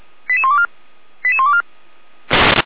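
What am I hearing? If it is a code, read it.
Five-tone selective call (ZVEI), heard as quick runs of five tones, tones of about 80 ms. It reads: 88124 88124
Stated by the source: ZVEI five-tone selective call received on an SDR.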